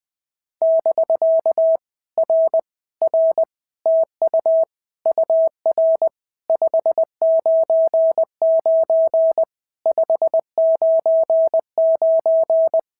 Morse code, heard as a keyed tone, about 20 wpm, 650 Hz. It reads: <BK> R R TU UR 599 599
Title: POTA QSO Practice - 20wpm